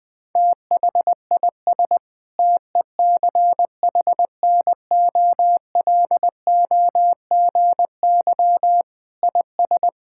Morse code, heard as a keyed tone, about 21 wpm, 700 Hz. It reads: THIS TECHNOLOGY IS
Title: Sentences with 8-Word Phrases - 20wpm